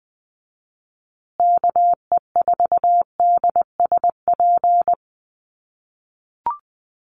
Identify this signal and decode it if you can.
Morse code, heard as a keyed tone, about 20 wpm, 700 Hz. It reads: KE4DSP
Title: USA Call Signs Repeated 3x 20wpm